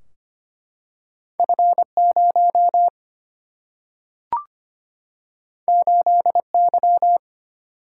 Morse code, heard as a keyed tone, about 25 wpm, 700 Hz. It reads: F0 8Y
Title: Single-Letter Single-Number 25wpm